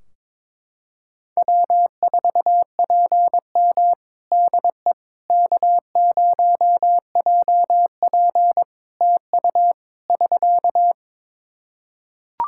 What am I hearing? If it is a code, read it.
Morse code, heard as a keyed tone, about 22 wpm, 700 Hz. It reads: W4PM DE K0JP TU <SK>
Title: Practice QSO - Exchange 4 - Ending - 22wpm